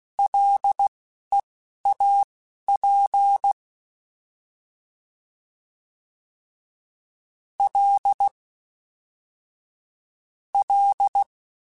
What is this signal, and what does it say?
Morse code, heard as a keyed tone, about 16 wpm, spaced as for 12 wpm, 800 Hz. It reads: LEAP L L